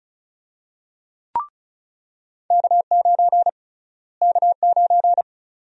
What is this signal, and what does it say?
Morse code, heard as a keyed tone, about 36 wpm, 700 Hz. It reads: K9 K9
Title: USA Call Sign Prefixes - Repeated 2x - 35wpm